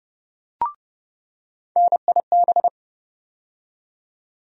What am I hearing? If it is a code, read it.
Morse code, heard as a keyed tone, about 30 wpm, 700 Hz. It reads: NIB
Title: USA Call Sign Postfixes - 3 Letters 30wpm